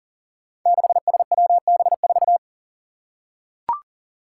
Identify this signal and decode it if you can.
Morse code, heard as a keyed tone, about 42 wpm, 700 Hz. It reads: 6SWB4
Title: Five Challenging Letter-Number Combinations 40wpm